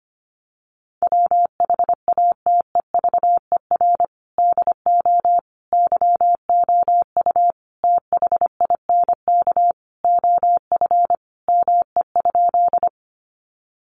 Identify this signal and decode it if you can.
Morse code, heard as a keyed tone, about 25 wpm, 700 Hz. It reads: WHATEVER DO YOU THINK OF ME?
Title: Sentences from Top 1100 Words 25wpm